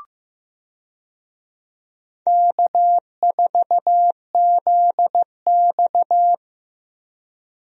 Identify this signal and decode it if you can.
Morse code, heard as a keyed tone, about 15 wpm, 700 Hz. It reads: K4ZX
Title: USA Call Signs - 4 Characters - Repeated 3x - 15wpm